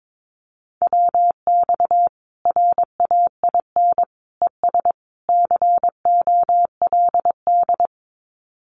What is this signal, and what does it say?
Morse code, heard as a keyed tone, about 22 wpm, 700 Hz. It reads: WX RAIN ES COLD
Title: Common QSO Phrases 22wpm